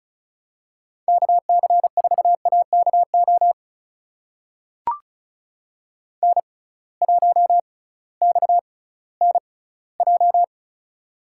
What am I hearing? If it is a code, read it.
Morse code, heard as a keyed tone, about 36 wpm, spaced as for 19 wpm, 700 Hz. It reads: KC4AKO N1XNJ
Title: USA Call Signs - 6x Character Spacing - 35wpm